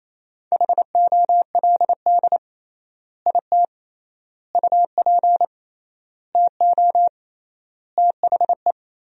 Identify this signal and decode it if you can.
Morse code, heard as a keyed tone, about 28 wpm, 700 Hz. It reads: HOLD IT UP TO THE
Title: 3x Word-Spacing - Sentences from Top 400 Words - 28wpm